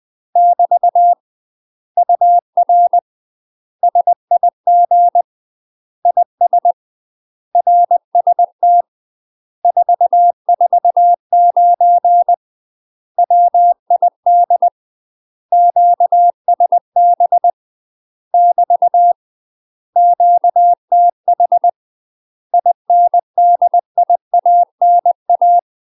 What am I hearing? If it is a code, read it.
Morse code, heard as a keyed tone, about 20 wpm, 700 Hz. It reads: <BT> UR SIG IS RST 449 WID QSB <BT> QTH INDIANA
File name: UK MASTER QSOs TRAINING / 20WPM / UK MASTER QSOs 0024